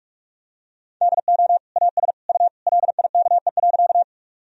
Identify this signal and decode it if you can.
Morse code, heard as a keyed tone, about 46 wpm, 700 Hz. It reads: DO AS U LIKE.
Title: Sentences from Top 100 Words and Abbreviations 45wpm